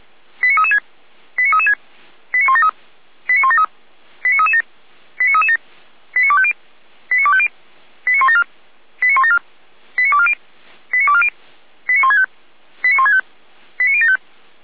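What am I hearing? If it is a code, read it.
Five-tone selective call (ZVEI), heard as quick runs of five tones, tones of about 70 ms. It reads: 89336 89336 89173 89173 89337 89337 89250 89250 89174 89174 89240 89240 89165 89165 89075
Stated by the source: ZVEI five-tone selective call 89108 received on SDR, from feuerwehr-orlamuende fire brigade server.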